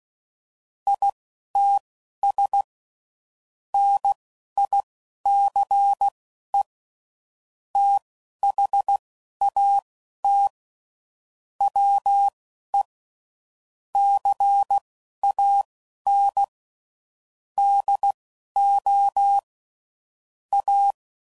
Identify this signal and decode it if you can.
Morse code, heard as a keyed tone, about 16 wpm, spaced as for 11 wpm, 800 Hz. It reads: ITS NICE THAT WE CAN DO A
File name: CW Code Groups @10WPM / QSOs / QSO WD0FBT